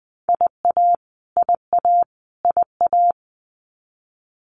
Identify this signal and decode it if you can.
Morse code, heard as a keyed tone, about 20 wpm, 700 Hz. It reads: IA IA IA
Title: US States Repeated 3x 20wpm